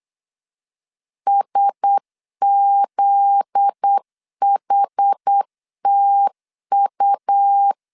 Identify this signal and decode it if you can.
Morse code, heard as a keyed tone, about 8 wpm, 800 Hz. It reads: SZHTU
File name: LEZIONI / 7 GF9Z / 7 GF9Z 8wpm 3213